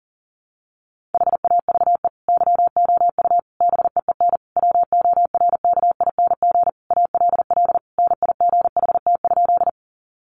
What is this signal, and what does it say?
Morse code, heard as a keyed tone, about 40 wpm, 700 Hz. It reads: HAVE YOU BEEN WORKING ALL NIGHT?
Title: Sentences from Top 700 Words 40wpm